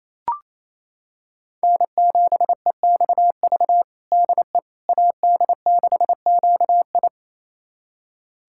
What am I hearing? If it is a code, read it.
Morse code, heard as a keyed tone, about 28 wpm, 700 Hz. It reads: N7EXV DE AD6QI
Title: USA Callsign DE Callsign - 5 Characters - 28wpm